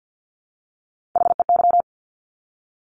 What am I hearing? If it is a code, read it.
Morse code, heard as a keyed tone, about 50 wpm, 700 Hz. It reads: HEY
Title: Top Three Letter Combinations in Words 50wpm